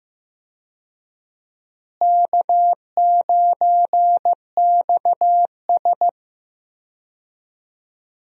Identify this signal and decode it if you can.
Morse code, heard as a keyed tone, about 16 wpm, 700 Hz. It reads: K9XS
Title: USA Call Signs - 4 Characters - Repeated 3x - 15wpm